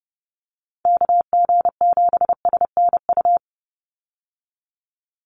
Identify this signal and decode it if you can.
Morse code, heard as a keyed tone, about 30 wpm, 700 Hz. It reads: KG7SNU
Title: USA Call Signs 30wpm